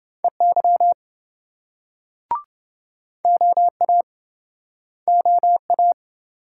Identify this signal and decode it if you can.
Morse code, heard as a keyed tone, about 28 wpm, 700 Hz. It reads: EY OA OA